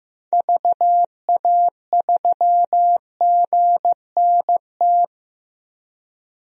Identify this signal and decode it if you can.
Morse code, heard as a keyed tone, about 15 wpm, 700 Hz. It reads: VA3GNT
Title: DXCC-100 Call Signs - 6 Characters - 15wpm